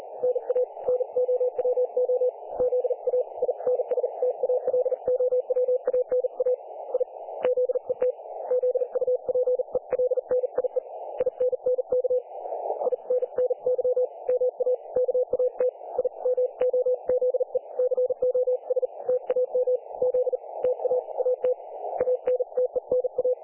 Morse code, heard as a keyed tone, about 40 wpm, spaced as for 35 wpm, 500 Hz. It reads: NA NOWOZAINSTALOWANA I ZET ZUPELNIE INNK INNY MA KAT I MOZE COS TAM Z TA AT ANTENA
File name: dlazaawansowanych / nagraniazpasma / websdr recording 3564.7kHz